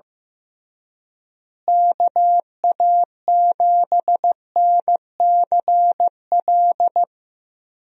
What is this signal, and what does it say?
Morse code, heard as a keyed tone, about 15 wpm, 700 Hz. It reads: KA7NCL
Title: USA Call Signs - 6 Characters - Repeated 3x - 15wpm